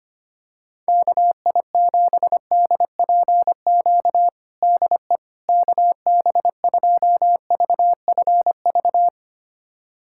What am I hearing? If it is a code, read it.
Morse code, heard as a keyed tone, about 25 wpm, 700 Hz. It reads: KI7DPQ DE KB2VFV
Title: USA Callsign DE Callsign - 6 Characters - 25wpm